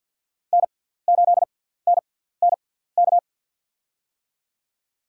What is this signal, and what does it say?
Morse code, heard as a keyed tone, about 53 wpm, spaced as for 17 wpm, 700 Hz. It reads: N8NNK
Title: USA Call Signs - 5 Characters - 6x Character Spacing - 50wpm